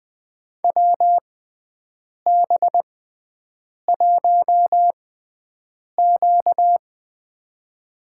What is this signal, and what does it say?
Morse code, heard as a keyed tone, about 20 wpm, 700 Hz. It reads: W B 1 Q
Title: USA Call Signs - 4 Characters - 6x Character Spacing - 20wpm